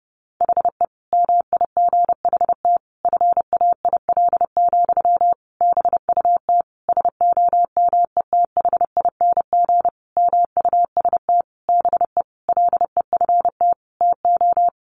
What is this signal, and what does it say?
Morse code, heard as a keyed tone, about 30 wpm, 700 Hz. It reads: HE MIGHT FAIL, BUT SOMETHING MUST BE LEFT TO
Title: Sentences from Top 1100 Words 30wpm